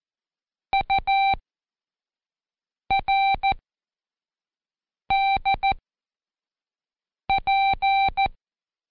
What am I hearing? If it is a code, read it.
Morse code, heard as a keyed tone, about 14 wpm, spaced as for 5 wpm, 780 Hz. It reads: URDP
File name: Code Quick / AudioDownld1 / Seg26-Day3-Review UDPR